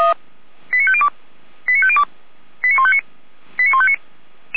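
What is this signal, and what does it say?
Five-tone selective call (ZVEI), heard as quick runs of five tones, tones of about 70 ms. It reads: 89502 89502 89159 89159
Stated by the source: ZVEI five-tone call received by SDR.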